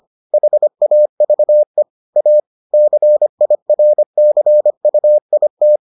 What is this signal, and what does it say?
Morse code, heard as a keyed tone, about 25 wpm, 600 Hz. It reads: HAVE A CIRCUIT